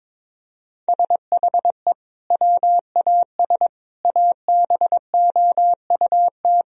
Morse code, heard as a keyed tone, about 22 wpm, 700 Hz. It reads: SHE WAS ABOUT
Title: Sentences from Top 1000 Words - Review Part 2 - 22wpm